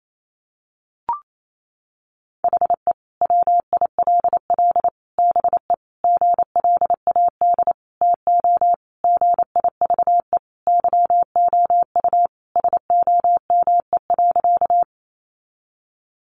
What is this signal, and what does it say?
Morse code, heard as a keyed tone, about 28 wpm, 700 Hz. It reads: HE WILL BE GLAD TO GIVE YOU SOME.
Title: Sentences from Top 800 Words 28wpm